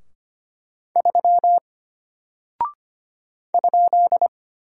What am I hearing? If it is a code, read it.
Morse code, heard as a keyed tone, about 25 wpm, 700 Hz. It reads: 3 ?
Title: Single Character (Letter:Number:Punctuation:Prosign) Alt 25wpm